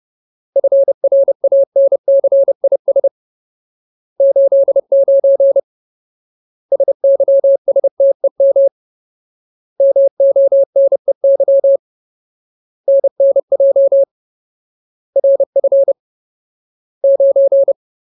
Morse code, heard as a keyed tone, about 30 wpm, 550 Hz. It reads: FRANCIS 89 SYSTEM MONEY NNJ RF 9
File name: Ultimate Word List / 30WPM Ultimate